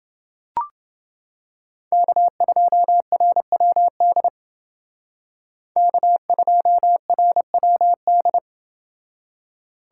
Morse code, heard as a keyed tone, about 29 wpm, 700 Hz. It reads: K2RWD K2RWD